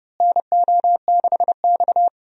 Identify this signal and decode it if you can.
Morse code, heard as a keyed tone, about 30 wpm, 700 Hz. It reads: NO6X
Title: USA Call Signs - 4 Characters - Repeated 3x - 30wpm